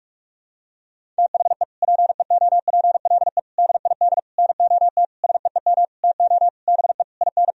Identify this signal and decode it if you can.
Morse code, heard as a keyed tone, about 42 wpm, 700 Hz. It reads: THE PEOPLE DID NOT SEEM TO BE IN
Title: Sentences from Top 1000 Words - Review Part 1 - 45wpm